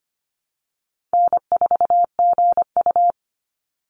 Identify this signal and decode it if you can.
Morse code, heard as a keyed tone, about 25 wpm, 700 Hz. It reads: N4GU
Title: Easy Call Signs - 25wpm